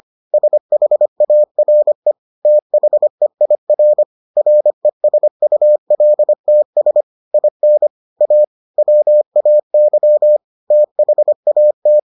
Morse code, heard as a keyed tone, about 25 wpm, 600 Hz. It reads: SHARE THEIR RESULTS IN A WAY THAT